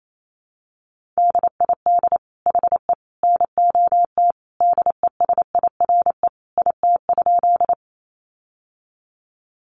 Morse code, heard as a keyed tone, about 28 wpm, 700 Hz. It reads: DID HE NOT DESIRE IT?
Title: Sentences from Top 1100 Words 28wpm